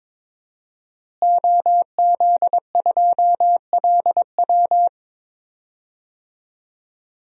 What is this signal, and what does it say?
Morse code, heard as a keyed tone, about 22 wpm, 700 Hz. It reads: OZ2LW
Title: DXCC-100 Call Signs - 5 Characters - Repeated 3x - 22wpm